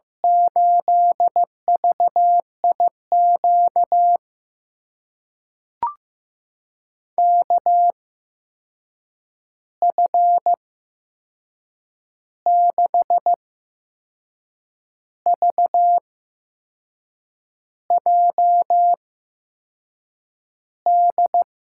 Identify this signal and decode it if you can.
Morse code, heard as a keyed tone, about 15 wpm, spaced as for 6 wpm, 700 Hz. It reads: 8VIQ KF6VJD
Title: USA Call Signs - 6 Characters - 8x Character Spacing - 15wpm